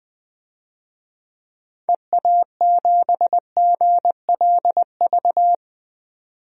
Morse code, heard as a keyed tone, about 20 wpm, 700 Hz. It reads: EA7GLV
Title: DXCC-100 Call Signs - 6 Characters - 20wpm